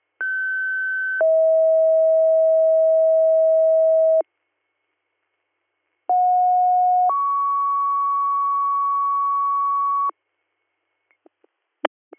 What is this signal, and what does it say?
Two-tone page, heard as a one-second tone, then a three-second tone, about 1550 then 640 Hz, about 730 then 1120 Hz. It is heard from 0.2 to 10.1 s.